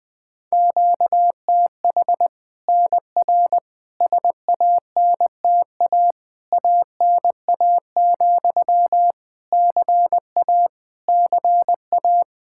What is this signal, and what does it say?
Morse code, heard as a keyed tone, about 20 wpm, 700 Hz. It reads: QTH NR SANTA ANA, CA CA
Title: Practice QSO - Exchange 1 - Beginning - 20wpm